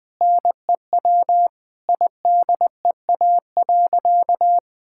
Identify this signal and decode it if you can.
Morse code, heard as a keyed tone, about 20 wpm, 700 Hz. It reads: NEW IDEA.
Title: Sentences with 2-Word Phrases - Spoken Beginning - 20wpm